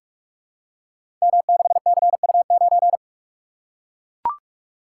Morse code, heard as a keyed tone, about 44 wpm, 700 Hz. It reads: M6CU9
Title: Code Groups - 5 Letter:Numbers - 45wpm